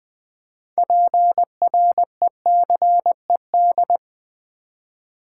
Morse code, heard as a keyed tone, about 20 wpm, 700 Hz. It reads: PRECED
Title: Build A Word - 3 Letters At A Time - 9-Letter Words - 20wpm